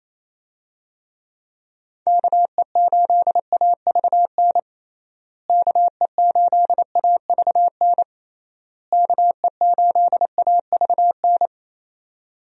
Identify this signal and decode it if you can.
Morse code, heard as a keyed tone, about 28 wpm, 700 Hz. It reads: KE8AVN KE8AVN KE8AVN
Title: USA Call Signs - 6 Characters - Repeated 3x - 28wpm